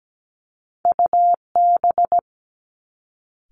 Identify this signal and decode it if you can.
Morse code, heard as a keyed tone, about 17 wpm, 700 Hz. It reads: UB
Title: Two Challenging Letter-Number Combinations 17wpm